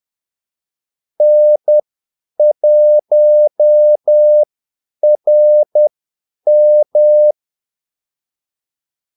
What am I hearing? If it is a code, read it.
Morse code, heard as a keyed tone, about 10 wpm, spaced as for 8 wpm, 600 Hz. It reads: N1RM
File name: Callsigns-10-8wpm